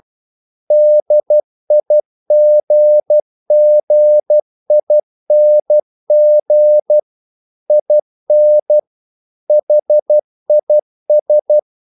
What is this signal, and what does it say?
Morse code, heard as a keyed tone, about 12 wpm, 600 Hz. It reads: DIGGING IN HIS
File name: mp3 files / the gardener - 12wpm